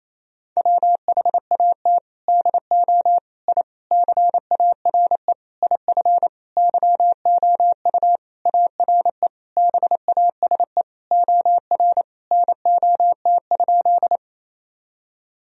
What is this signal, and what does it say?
Morse code, heard as a keyed tone, about 27 wpm, 700 Hz. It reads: WHAT DO I CARE IF YOU ARE BASE OR NOT?
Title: Sentences from Top 1000 Words - Review Part 1 - 28wpm